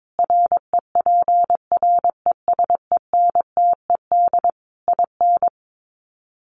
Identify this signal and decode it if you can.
Morse code, heard as a keyed tone, about 22 wpm, 700 Hz. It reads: REPRESENTED IN
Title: Sets of 5 Words 22wpm